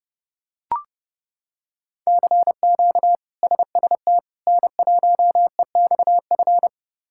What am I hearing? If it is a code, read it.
Morse code, heard as a keyed tone, about 31 wpm, 700 Hz. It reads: CQ SST N1EXF
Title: K1USN SST QSO Practice - 30wpm